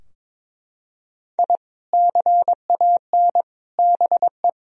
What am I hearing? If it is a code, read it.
Morse code, heard as a keyed tone, about 22 wpm, 700 Hz. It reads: I CAN BE
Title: Sentences from Top 1000 Words - Review Part 3 - 22wpm